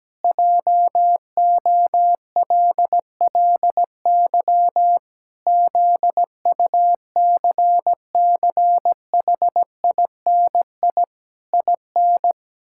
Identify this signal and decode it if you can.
Morse code, heard as a keyed tone, about 17 wpm, 700 Hz. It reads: JOLLY ZUCCHINI IN
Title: Pangram Challenge v2 - 17wpm